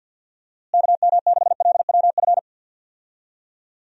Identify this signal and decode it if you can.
Morse code, heard as a keyed tone, about 50 wpm, 700 Hz. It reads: KM6LWF
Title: USA Call Signs - 6 Characters - Repeated 2x - 50wpm